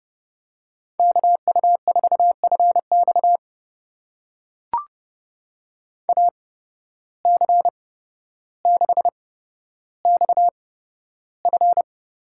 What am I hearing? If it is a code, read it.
Morse code, heard as a keyed tone, about 30 wpm, spaced as for 13 wpm, 700 Hz. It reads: KU4FX AC6XF
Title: USA Call Signs - 5 Characters - 8x Character Spacing - 30wpm